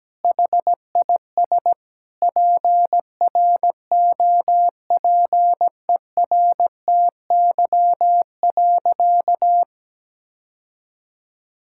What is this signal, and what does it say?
Morse code, heard as a keyed tone, about 17 wpm, 700 Hz. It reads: HIS PROPERTY.
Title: Sentences from Top 1600 Words 17wpm